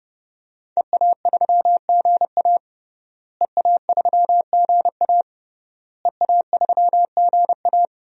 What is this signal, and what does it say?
Morse code, heard as a keyed tone, about 29 wpm, 700 Hz. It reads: EA3GA EA3GA EA3GA
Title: DXCC-100 Call Signs - 5 Characters - Repeated 3x - 30wpm